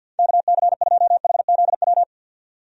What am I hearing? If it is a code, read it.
Morse code, heard as a keyed tone, about 51 wpm, 700 Hz. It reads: KC2HZW
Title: USA Call Signs - 6 Characters - 50wpm